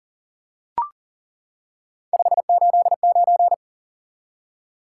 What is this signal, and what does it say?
Morse code, heard as a keyed tone, about 40 wpm, 700 Hz. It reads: 589
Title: Top 180 QSO Elements 40wpm